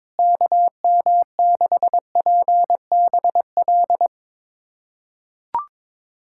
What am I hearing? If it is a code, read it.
Morse code, heard as a keyed tone, about 22 wpm, 700 Hz. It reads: KM6PBL
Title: USA Callsign DE Callsign - 6 Characters - 22wpm